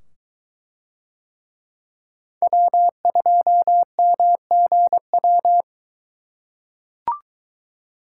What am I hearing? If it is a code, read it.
Morse code, heard as a keyed tone, about 21 wpm, 700 Hz. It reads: W2MGW